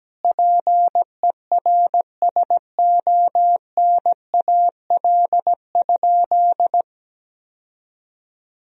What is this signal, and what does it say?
Morse code, heard as a keyed tone, about 17 wpm, 700 Hz. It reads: PERSONAL?
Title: Sentences from Top 1000 Words - Review Part 4 - 17wpm